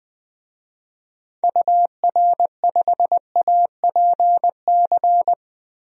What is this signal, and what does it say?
Morse code, heard as a keyed tone, about 21 wpm, 700 Hz. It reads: UR5APC
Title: DXCC-100 Call Signs - 6 Characters - Repeated 2x - 20wpm